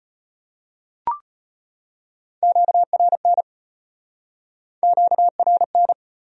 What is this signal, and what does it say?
Morse code, heard as a keyed tone, about 38 wpm, 700 Hz. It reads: QRN QRN